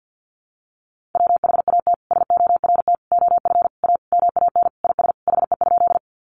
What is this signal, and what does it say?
Morse code, heard as a keyed tone, about 50 wpm, 700 Hz. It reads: WHAT SORT OF A MAN IS HE?
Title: Sentences from Top 1300 Words 50wpm